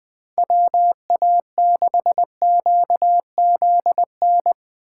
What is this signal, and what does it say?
Morse code, heard as a keyed tone, about 23 wpm, 700 Hz. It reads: WA6QZN
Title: Speed Racing - USA 2x3 Call Signs 20wpm